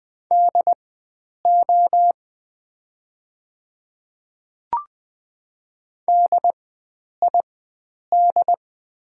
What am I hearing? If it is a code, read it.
Morse code, heard as a keyed tone, about 20 wpm, spaced as for 9 wpm, 690 Hz. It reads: DO DID
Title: 4x Character Spacing - Top 100 Words - 20wpm